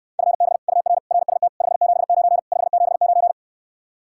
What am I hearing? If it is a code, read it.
Morse code, heard as a keyed tone, about 69 wpm, 700 Hz. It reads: FB UR RST 579 579